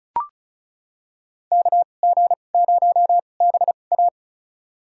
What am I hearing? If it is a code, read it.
Morse code, heard as a keyed tone, about 34 wpm, spaced as for 25 wpm, 700 Hz. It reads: KG0BA
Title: USA Call Signs - 5 Characters - 2x Character Spacing - 35wpm